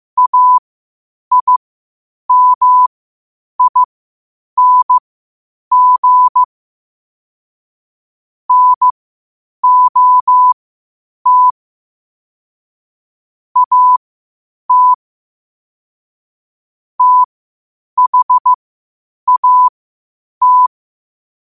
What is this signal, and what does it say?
Morse code, heard as a keyed tone, about 15 wpm, spaced as for 8 wpm, 1000 Hz. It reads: AIMING NOT AT THAT
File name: qotdfiles / 8wpm